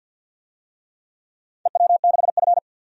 Morse code, heard as a keyed tone, about 51 wpm, 700 Hz. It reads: EW6F
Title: DXCC-100 Call Signs - 4 Characters - 50wpm